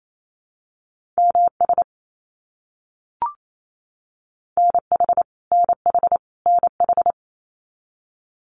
Morse code, heard as a keyed tone, about 28 wpm, 700 Hz. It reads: MS NH NH NH
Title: US States Repeated 3x 28wpm